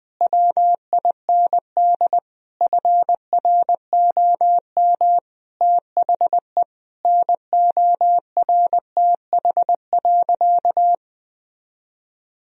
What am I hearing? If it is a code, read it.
Morse code, heard as a keyed tone, about 20 wpm, 700 Hz. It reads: WIND FROM THE NORTH.
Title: Sentences from Top 1000 Words - Review Part 2 - 20wpm